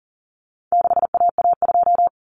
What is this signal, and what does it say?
Morse code, heard as a keyed tone, about 40 wpm, 700 Hz. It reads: 6AA2
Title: Common Cut Numbers - 4-Digit 40wpm